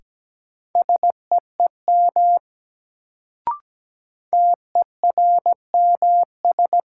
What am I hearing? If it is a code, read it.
Morse code, heard as a keyed tone, about 17 wpm, 700 Hz. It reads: SEEM TERMS
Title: Sets of 3 Words - New Word Drill - 17wpm